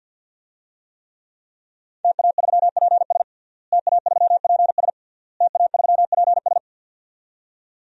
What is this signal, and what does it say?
Morse code, heard as a keyed tone, about 53 wpm, 700 Hz. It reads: TA3PS TA3PS TA3PS
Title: DXCC-100 Call Signs - 5 Characters - Repeated 3x - 50wpm